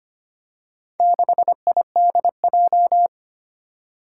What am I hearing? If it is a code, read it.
Morse code, heard as a keyed tone, about 25 wpm, 700 Hz. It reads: 6IDJ
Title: Four Challenging Letter-Number Combinations 25wpm